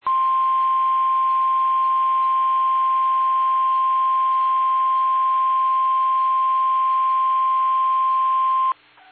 Emergency Alert System (SAME) tones (the 1050 Hz attention tone). It is heard from 0.0 to 8.7 s.
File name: eas / KBBX / KBBX